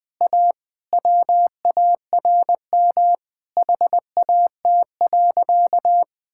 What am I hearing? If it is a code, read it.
Morse code, heard as a keyed tone, about 20 wpm, 700 Hz. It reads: A WARM HAT.
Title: Easy Sentences with Hints - 20wpm